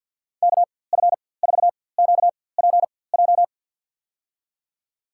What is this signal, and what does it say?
Morse code, heard as a keyed tone, about 46 wpm, spaced as for 23 wpm, 700 Hz. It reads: KF4QPJ
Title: USA Call Signs - 6 Characters - 4x Character Spacing - 50wpm